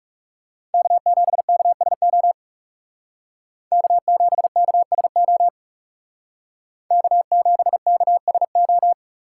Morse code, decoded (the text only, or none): K7KSO K7KSO K7KSO